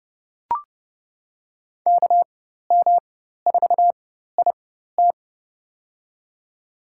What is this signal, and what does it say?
Morse code, heard as a keyed tone, about 31 wpm, spaced as for 14 wpm, 700 Hz. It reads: KM4IT
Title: USA Call Signs - 5 Characters - 4x Character Spacing - 30wpm